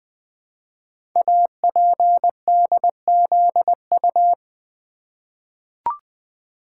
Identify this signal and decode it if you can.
Morse code, heard as a keyed tone, about 20 wpm, 700 Hz. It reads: APDZU
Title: Code Groups - 5 Letters - 20wpm